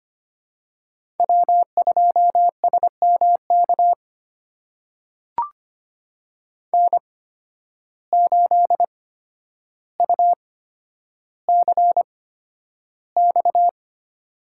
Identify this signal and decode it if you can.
Morse code, heard as a keyed tone, about 25 wpm, spaced as for 11 wpm, 700 Hz. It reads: W2SMK N8UCX
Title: USA Call Signs - 5 Characters - 8x Character Spacing - 25wpm